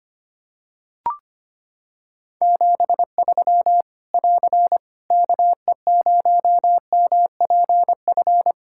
Morse code, heard as a keyed tone, about 25 wpm, 700 Hz. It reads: 73 <AR> KE0MPF
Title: Practice QSO - Exchange 4 - Ending - 25wpm